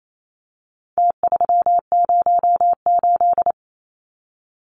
Morse code, heard as a keyed tone, about 28 wpm, 700 Hz. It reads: T308
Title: Numbers - Contest Style - 4-Digit 28wpm